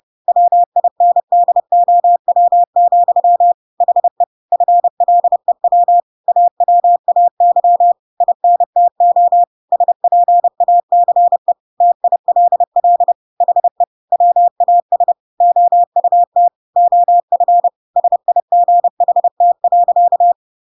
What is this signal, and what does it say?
Morse code, heard as a keyed tone, about 30 wpm, 700 Hz. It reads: WINDOW, HE FLEW AWAY INTO SPACE TILL HE WAS OUT OF SIGHT.
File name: Pinocchio / Pinocchio ch5 30 30 ik1hge